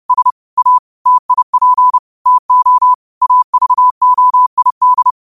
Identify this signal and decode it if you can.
Morse code, heard as a keyed tone, about 30 wpm, 1000 Hz. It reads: S A TIP TO AVOID
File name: 30wpm